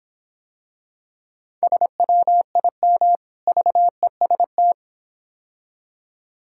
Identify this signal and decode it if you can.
Morse code, heard as a keyed tone, about 26 wpm, 700 Hz. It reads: SWIM VEST